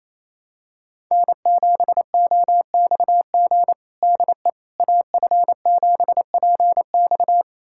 Morse code, heard as a keyed tone, about 28 wpm, 700 Hz. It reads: N7OXG DE AF7PX
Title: USA Callsign DE Callsign - 5 Characters - 28wpm